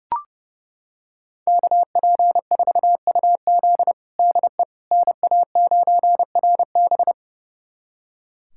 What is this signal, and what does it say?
Morse code, heard as a keyed tone, about 30 wpm, 700 Hz. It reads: KP4UZ DE NA9RB
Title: USA Callsign DE Callsign - 5 Characters - 30wpm